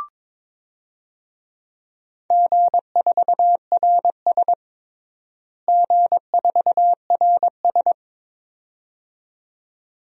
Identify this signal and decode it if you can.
Morse code, heard as a keyed tone, about 22 wpm, 700 Hz. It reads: G4RS G4RS
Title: DXCC-100 Call Signs - 4 Characters - Repeated 2x - 22wpm